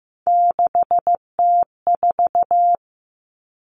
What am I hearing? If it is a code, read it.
Morse code, heard as a keyed tone, about 15 wpm, 700 Hz. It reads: 6T4
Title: Three Challenging Letter-Number Combinations 15wpm